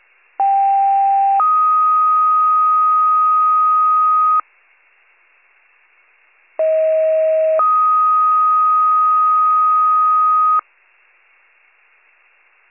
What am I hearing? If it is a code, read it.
Two-tone page, heard as a one-second tone, then a three-second tone, about 790 then 1250 Hz, about 640 then 1250 Hz. It is heard from 0.4 to 10.6 s.